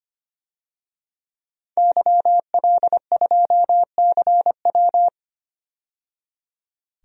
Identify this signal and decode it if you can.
Morse code, heard as a keyed tone, about 25 wpm, 690 Hz. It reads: YL2CW
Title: DXCC-100 Call Signs - 5 Characters - 25wpm